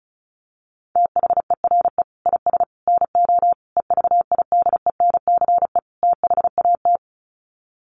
Morse code, heard as a keyed tone, about 34 wpm, 700 Hz. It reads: THERE IS NO EVIDENCE THAT
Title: Sets of 5 Words Encore 35wpm